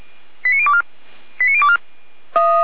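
Five-tone selective call (ZVEI), heard as quick runs of five tones, tones of about 70 ms. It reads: 89024 89024
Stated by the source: ZVEI five-tone selective call received on an SDR.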